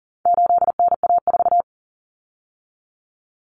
Morse code, heard as a keyed tone, about 40 wpm, 700 Hz. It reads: 8NA4
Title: Common Cut Numbers - 4-Digit 40wpm